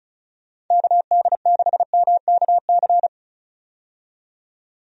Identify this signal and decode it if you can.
Morse code, heard as a keyed tone, about 34 wpm, 700 Hz. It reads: KD6MKC
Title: USA Call Signs - 6 Characters - 35wpm